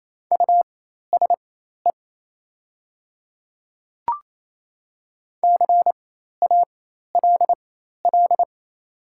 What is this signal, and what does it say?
Morse code, heard as a keyed tone, about 28 wpm, spaced as for 13 wpm, 700 Hz. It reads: USE CALL
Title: 4x Character Spacing - Top 100 Words - 28wpm